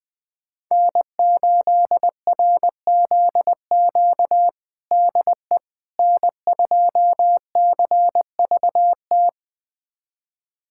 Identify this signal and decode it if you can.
Morse code, heard as a keyed tone, about 21 wpm, 700 Hz. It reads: N8RZQ DE N2CVT
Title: USA Callsign DE Callsign - 5 Characters - 20wpm